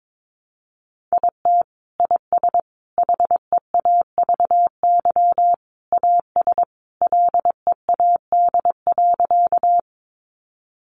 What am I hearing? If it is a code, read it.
Morse code, heard as a keyed tone, about 22 wpm, 700 Hz. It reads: IT IS HEAVY AS LEAD.
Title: Sentences from Top 1100 Words 22wpm